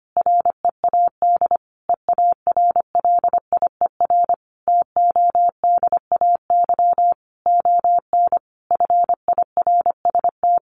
Morse code, heard as a keyed tone, about 25 wpm, 700 Hz. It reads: READ EARLIER TODAY ON FIRST
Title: A Sound Of Thunder 25wpm